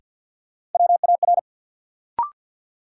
Morse code, heard as a keyed tone, about 53 wpm, 700 Hz. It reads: WAR
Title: Sentences from Top 400 Words - New Word Drill - 50wpm